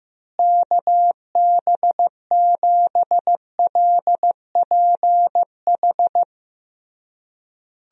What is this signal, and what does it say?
Morse code, heard as a keyed tone, about 15 wpm, 700 Hz. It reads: KB7LPH
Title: USA Call Signs - 6 Characters - Repeated 3x - 15wpm